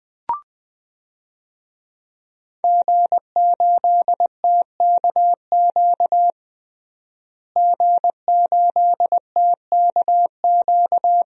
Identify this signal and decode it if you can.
Morse code, heard as a keyed tone, about 20 wpm, 700 Hz. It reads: G8TKQ G8TKQ
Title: DXCC-100 Call Signs - 5 Characters - Repeated 2x - 20wpm